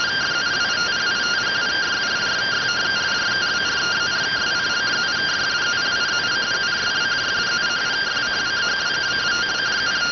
Two-tone teleprinter signal, about 170 Hz shift.